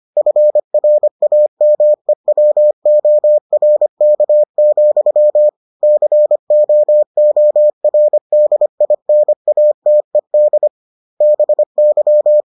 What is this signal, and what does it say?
Morse code, heard as a keyed tone, about 25 wpm, 600 Hz. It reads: FRAMEWORK, COORDINATED BY